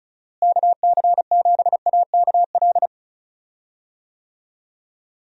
Morse code, heard as a keyed tone, about 36 wpm, 700 Hz. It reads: KC7AKL
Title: USA Call Signs - 6 Characters - Repeated 2x - 35wpm